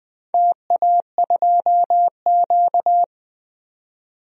Morse code, heard as a keyed tone, about 20 wpm, 700 Hz. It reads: TA2Q
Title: DXCC-100 Call Signs - 4 Characters - 20wpm